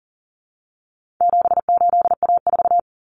Morse code, heard as a keyed tone, about 40 wpm, 700 Hz. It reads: 78A4
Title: Common Cut Numbers - 4-Digit 40wpm